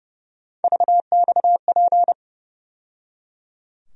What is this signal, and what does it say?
Morse code, heard as a keyed tone, about 30 wpm, 700 Hz. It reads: VXP
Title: USA Call Sign Postfixes - 3 Letters 30wpm